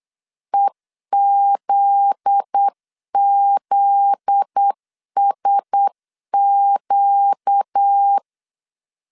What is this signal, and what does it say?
Morse code, heard as a keyed tone, about 8 wpm, 800 Hz. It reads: EZZSQ
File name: CW / DET / 1 9 1500C S A 8wpm 5637